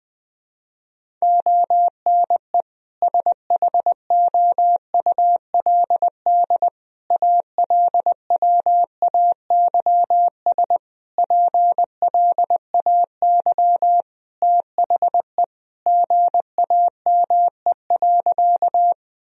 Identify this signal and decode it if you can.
Morse code, heard as a keyed tone, about 20 wpm, 700 Hz. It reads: ONE SHOULD ALWAYS PLAY THE GAME.
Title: Sentences from Top 1000 Words - Review Part 3 - 20wpm